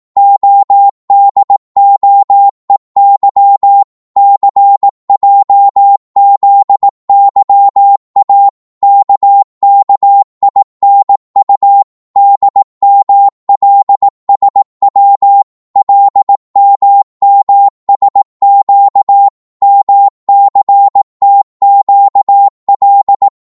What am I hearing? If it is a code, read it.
Morse code, heard as a keyed tone, about 18 wpm, 800 Hz. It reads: ODOEY CJZYA KKINU DMLSW LMMSQ MCTQL
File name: Morse / ltr-18wpm